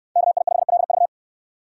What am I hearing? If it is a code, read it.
Morse code, heard as a keyed tone, about 65 wpm, 700 Hz. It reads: KI5LV